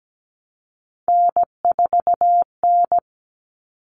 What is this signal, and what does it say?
Morse code, heard as a keyed tone, about 17 wpm, 700 Hz. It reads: N4N
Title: Common Cut Numbers - 3-Digit 17wpm